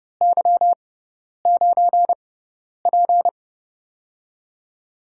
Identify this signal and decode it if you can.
Morse code, heard as a keyed tone, about 30 wpm, spaced as for 10 wpm, 700 Hz. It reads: Y9P
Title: USA Call Signs - 6x Character Spacing - 30wpm